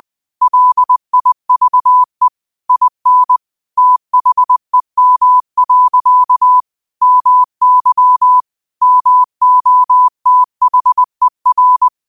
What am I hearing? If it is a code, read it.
Morse code, heard as a keyed tone, about 20 wpm, 1000 Hz. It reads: LIVE IN THEM. MY MOTHER
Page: New quotes every day in morse code at 20 Words per minute.